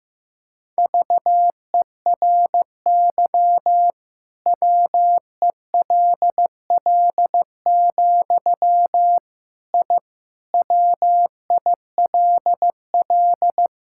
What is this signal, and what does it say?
Morse code, heard as a keyed tone, about 15 wpm, 700 Hz. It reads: VERY WELL, I WILL
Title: Sentences from Top 1000 Words - Review Part 5 - 15wpm